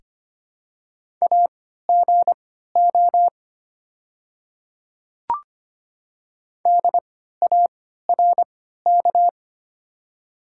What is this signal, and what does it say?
Morse code, heard as a keyed tone, about 25 wpm, spaced as for 14 wpm, 700 Hz. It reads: AGO DARK
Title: 3x Character Spacing - Top 500 Words - 25wpm